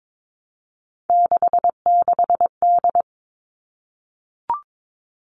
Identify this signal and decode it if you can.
Morse code, heard as a keyed tone, about 22 wpm, 700 Hz. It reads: 66D
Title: All Cut Numbers - 3-Digit 22wpm